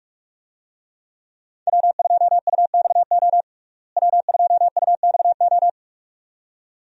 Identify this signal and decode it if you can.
Morse code, heard as a keyed tone, about 46 wpm, 700 Hz. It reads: W2UXO W2UXO
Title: USA Call Signs - 5 Characters - Repeated 2x - 45wpm